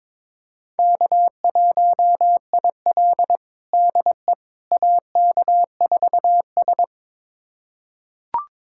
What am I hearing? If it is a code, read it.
Morse code, heard as a keyed tone, about 22 wpm, 700 Hz. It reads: K1IL DE AK4S
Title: USA Callsign DE Callsign - 4 Characters - 22wpm